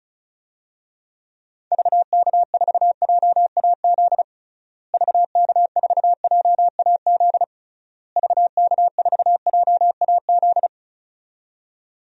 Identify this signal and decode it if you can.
Morse code, heard as a keyed tone, about 36 wpm, 700 Hz. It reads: VK4JAZ VK4JAZ VK4JAZ
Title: DXCC-100 Call Signs - 6 Characters - Repeated 3x - 35wpm